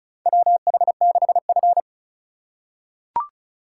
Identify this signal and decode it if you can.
Morse code, heard as a keyed tone, about 36 wpm, 700 Hz. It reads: WH6F
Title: USA Call Signs - 4 Characters - 3x Character Spacing - 35wpm